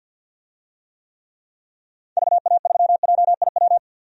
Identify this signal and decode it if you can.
Morse code, heard as a keyed tone, about 45 wpm, 700 Hz. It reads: VA3JIW
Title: DXCC-100 Call Signs - 6 Characters - 50wpm